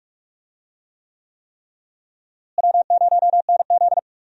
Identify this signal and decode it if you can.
Morse code, heard as a keyed tone, about 46 wpm, 700 Hz. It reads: W0NZ
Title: USA Call Signs - 4 Characters - Repeated 3x - 45wpm